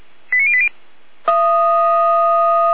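Five-tone selective call (ZVEI), heard as quick runs of five tones, tones of about 70 ms. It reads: 89080
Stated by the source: ZVEI five-tone call 89117 received by SDR.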